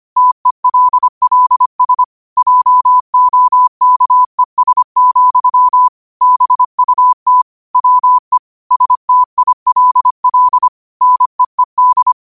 Morse code, heard as a keyed tone, about 25 wpm, 1000 Hz. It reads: TELLS JOKES, BUT WE STILL NEED
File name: qotdfiles / 25wpm